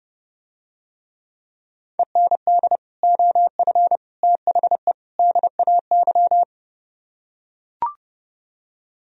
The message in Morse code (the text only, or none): END OF THE DAY